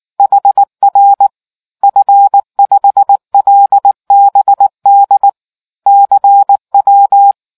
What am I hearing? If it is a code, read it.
Morse code, heard as a keyed tone, about 19 wpm, 800 Hz. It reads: HR F5LBD CW